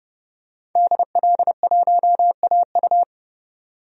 Morse code, heard as a keyed tone, about 30 wpm, 700 Hz. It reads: DL1AU
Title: DXCC-100 Call Signs - 5 Characters - 30wpm